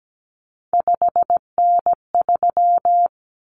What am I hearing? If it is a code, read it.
Morse code, heard as a keyed tone, about 17 wpm, 700 Hz. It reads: 5N3
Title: Common Cut Numbers - 3-Digit 17wpm